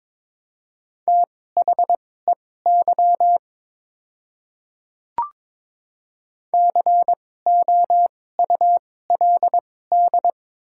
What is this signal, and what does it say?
Morse code, heard as a keyed tone, about 22 wpm, spaced as for 16 wpm, 700 Hz. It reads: THEY COULD
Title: 2x Character Spacing - Top 100 Words - 22wpm